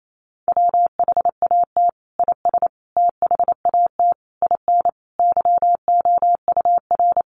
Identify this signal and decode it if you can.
Morse code, heard as a keyed tone, about 28 wpm, 700 Hz. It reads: WHAT IS THAT IN YOUR
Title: Sentences from Top 200 Words 28wpm